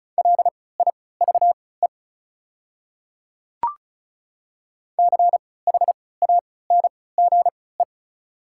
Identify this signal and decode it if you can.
Morse code, heard as a keyed tone, about 34 wpm, spaced as for 20 wpm, 700 Hz. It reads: LIVE CHANGE
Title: 3x Character Spacing - Top 200 Words - 35wpm